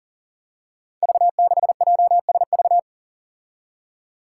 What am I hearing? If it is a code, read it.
Morse code, heard as a keyed tone, about 40 wpm, 700 Hz. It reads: V6JSV
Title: Five Challenging Letter-Number Combinations 40wpm